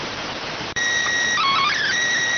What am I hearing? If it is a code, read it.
SSTV (slow-scan television), PD 290.